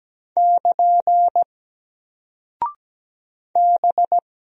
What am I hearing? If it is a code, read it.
Morse code, heard as a keyed tone, about 17 wpm, 700 Hz. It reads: <KN> B
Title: Single Character (Letter:Number:Punctuation:Prosign) Alt 17wpm